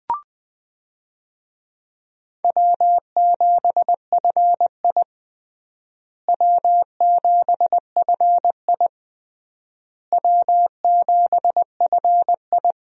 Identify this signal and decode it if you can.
Morse code, heard as a keyed tone, about 20 wpm, 700 Hz. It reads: W7FI W7FI W7FI
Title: USA Call Signs - 4 Characters - Repeated 3x - 20wpm